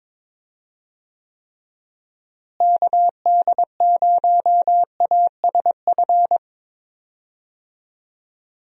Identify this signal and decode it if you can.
Morse code, heard as a keyed tone, about 22 wpm, 700 Hz. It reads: KD0ASF
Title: USA Call Signs - 6 Characters - Repeated 3x - 22wpm